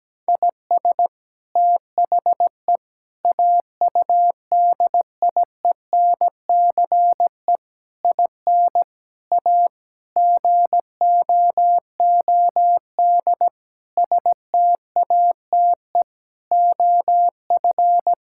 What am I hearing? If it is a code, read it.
Morse code, heard as a keyed tone, about 17 wpm, 700 Hz. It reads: IS THE AUDIENCE IN A GOOD STATE OF
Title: Sentences from Top 1000 Words - Review Part 5 - 17wpm